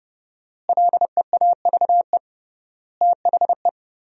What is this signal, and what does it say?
Morse code, heard as a keyed tone, about 30 wpm, 700 Hz. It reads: LEAVE THE
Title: 3x Word-Spacing - Sentences from Top 400 Words - 30wpm